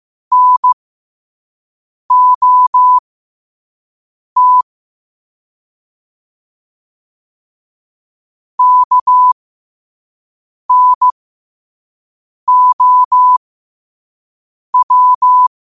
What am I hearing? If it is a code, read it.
Morse code, heard as a keyed tone, about 15 wpm, spaced as for 5 wpm, 1000 Hz. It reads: NOT KNOW
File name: qotdfiles / 5wpm